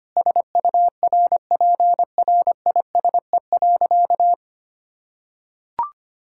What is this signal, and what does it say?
Morse code, heard as a keyed tone, about 25 wpm, 700 Hz. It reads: SURPRISE.
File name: practice / 25wpm / Sentences with 4-Word Phrases - 25wpm